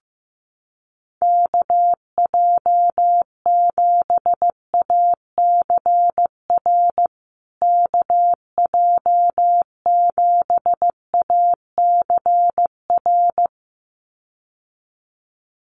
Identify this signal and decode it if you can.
Morse code, heard as a keyed tone, about 15 wpm, 700 Hz. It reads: KJ7ACR KJ7ACR
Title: USA Call Signs Repeated 2x 15wpm